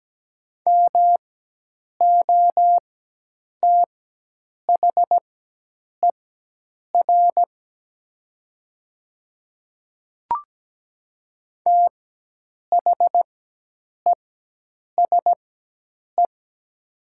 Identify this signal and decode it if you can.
Morse code, heard as a keyed tone, about 17 wpm, spaced as for 8 wpm, 700 Hz. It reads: MOTHER THESE
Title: 4x Character Spacing - Top 200 Words - 17wpm